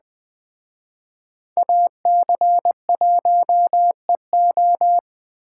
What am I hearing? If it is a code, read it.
Morse code, heard as a keyed tone, about 20 wpm, 700 Hz. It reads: AC1EO